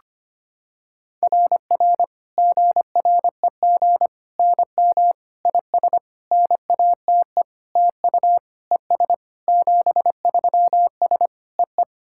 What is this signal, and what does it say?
Morse code, heard as a keyed tone, about 25 wpm, 700 Hz. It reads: RR GREG NM IS NATE TU ES 73S EE
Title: SOTA QSO Practice 25wpm